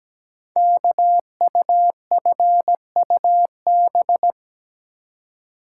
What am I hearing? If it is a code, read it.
Morse code, heard as a keyed tone, about 17 wpm, 700 Hz. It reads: KUFUB
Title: Code Groups - 5 Letters - 17wpm